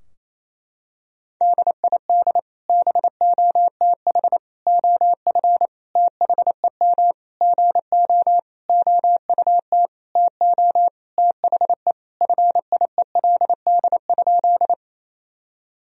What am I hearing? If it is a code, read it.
Morse code, heard as a keyed tone, about 28 wpm, 700 Hz. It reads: DID BOTH OF THEM GO OUT TO THE FIELD?
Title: Sentences from Top 1000 Words - Review Part 2 - 28wpm